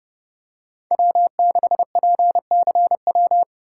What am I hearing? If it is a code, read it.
Morse code, heard as a keyed tone, about 32 wpm, 700 Hz. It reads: W6PCW
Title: USA Call Signs - 3x Character Spacing - 30wpm